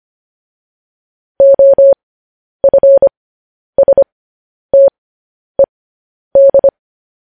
Morse code, heard as a keyed tone, about 25 wpm, spaced as for 10 wpm, 550 Hz. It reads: OFSTED
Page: news-headlines-in-morse-code-at-10-wpm